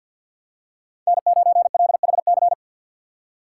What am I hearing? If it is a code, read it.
Morse code, heard as a keyed tone, about 53 wpm, 700 Hz. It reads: N9LHC
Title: USA Call Signs - 5 Characters - 3x Character Spacing - 50wpm